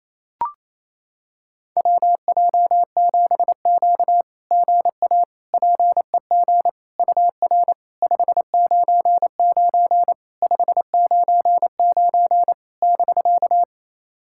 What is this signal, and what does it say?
Morse code, heard as a keyed tone, about 29 wpm, 690 Hz. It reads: WJ7Q GA PEG UR 599 599 <BK>
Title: SOTA QSO Practice 28wpm